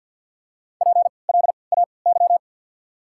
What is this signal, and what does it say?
Morse code, heard as a keyed tone, about 53 wpm, spaced as for 29 wpm, 690 Hz. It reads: PLAY
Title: 3x Character Spacing - Top 200 Words - 50wpm